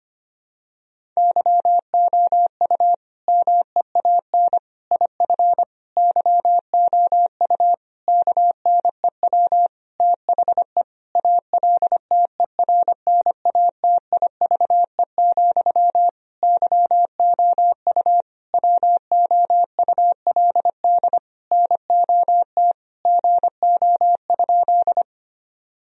Morse code, heard as a keyed tone, about 25 wpm, 700 Hz. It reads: YOU MEAN IF YOU KNEW THE ALTERNATIVE, YOU WOULD NOT GO?
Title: Sentences from Top 1000 Words - Review Part 4 - 25wpm